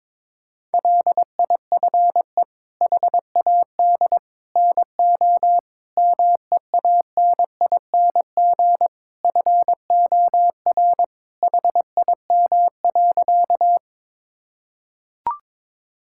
Morse code, heard as a keyed tone, about 22 wpm, 700 Hz. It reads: LIFE HAD NO MEANING FOR HIM.
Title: Sentences from Top 1000 Words - Review Part 6 - 22wpm